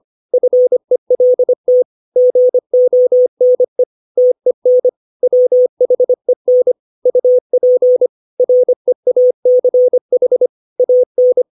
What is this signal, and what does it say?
Morse code, heard as a keyed tone, about 25 wpm, 500 Hz. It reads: FELT GONE TEN WHEN UP REACH AN
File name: 632 words / 632 Words 25WPM